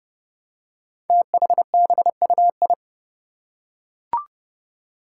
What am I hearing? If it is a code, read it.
Morse code, heard as a keyed tone, about 30 wpm, 700 Hz. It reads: THBUI
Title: Five Challenging Letter-Number Combinations 30wpm